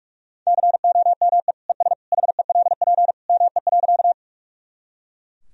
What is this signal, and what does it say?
Morse code, heard as a keyed tone, about 43 wpm, 700 Hz. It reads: COME ES HELP ME.
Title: Sentences from Top 200 Words and Abbreviations 45wpm